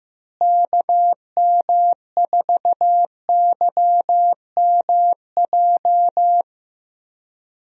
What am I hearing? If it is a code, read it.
Morse code, heard as a keyed tone, about 15 wpm, 700 Hz. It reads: KM4YMJ
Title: USA Call Signs - 6 Characters - Repeated 2x - 15wpm